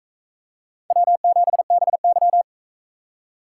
Morse code, heard as a keyed tone, about 40 wpm, 700 Hz. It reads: W7BY